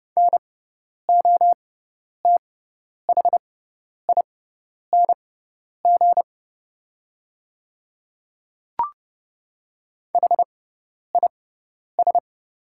Morse code, heard as a keyed tone, about 30 wpm, spaced as for 10 wpm, 700 Hz. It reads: NOTHING HIS
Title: 6x Character Spacing - Top 500 Words - 30wpm